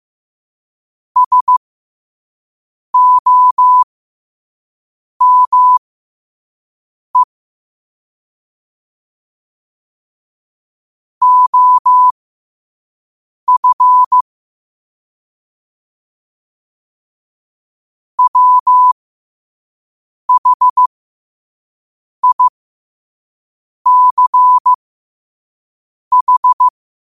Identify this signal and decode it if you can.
Morse code, heard as a keyed tone, about 15 wpm, spaced as for 5 wpm, 1000 Hz. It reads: SOME OF WHICH
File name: qotdfiles / 5wpm